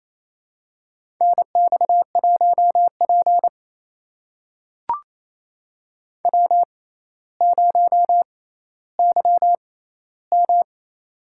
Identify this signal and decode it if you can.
Morse code, heard as a keyed tone, about 27 wpm, spaced as for 14 wpm, 700 Hz. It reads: NX1P W0YM
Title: USA Call Signs - 4 Characters - 6x Character Spacing - 28wpm